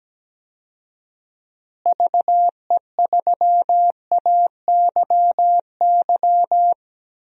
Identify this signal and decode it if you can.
Morse code, heard as a keyed tone, about 17 wpm, 700 Hz. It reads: VE3AYY
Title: DXCC-100 Call Signs - 6 Characters - Repeated 3x - 17wpm